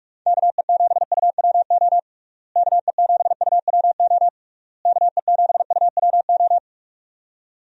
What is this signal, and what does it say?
Morse code, heard as a keyed tone, about 48 wpm, 700 Hz. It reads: KE7UWO KE7UWO KE7UWO
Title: USA Call Signs - 6 Characters - Repeated 3x - 45wpm